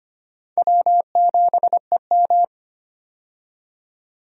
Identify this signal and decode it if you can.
Morse code, heard as a keyed tone, about 25 wpm, 700 Hz. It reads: W7EM
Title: USA Call Signs - 4 Characters - 25wpm